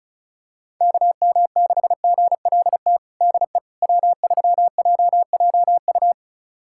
Morse code, heard as a keyed tone, about 36 wpm, 690 Hz. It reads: KM6GLT DE W3JJU
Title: USA Callsign DE Callsign 35wpm